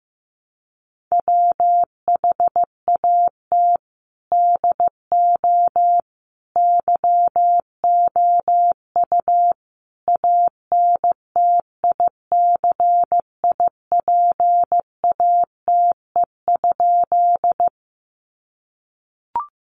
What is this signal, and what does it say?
Morse code, heard as a keyed tone, about 15 wpm, 700 Hz. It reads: WHAT DO YOU ANTICIPATE?
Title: Sentences from Top 1500 Words 15wpm